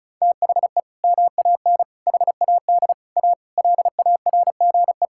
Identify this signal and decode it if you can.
Morse code, heard as a keyed tone, about 34 wpm, 700 Hz. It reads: THE MAN HAD A LARGE
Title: Sentences from Top 1000 Words - Review Part 6 - 35wpm